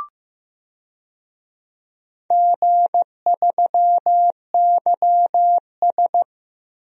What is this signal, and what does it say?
Morse code, heard as a keyed tone, about 15 wpm, 700 Hz. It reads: G3YS
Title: DXCC-100 Call Signs - 4 Characters - Repeated 2x - 15wpm